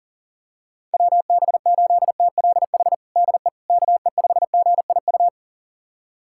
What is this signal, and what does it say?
Morse code, heard as a keyed tone, about 40 wpm, 700 Hz. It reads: WB8TLH DE KE5GIU
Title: USA Callsign DE Callsign - 6 Characters - 40wpm